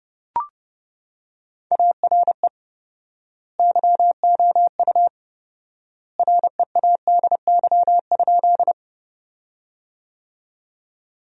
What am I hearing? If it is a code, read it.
Morse code, heard as a keyed tone, about 31 wpm, 700 Hz. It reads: ARE YOU READY?
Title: 4x Word-Spacing - Sentences from Top 400 Words - 30wpm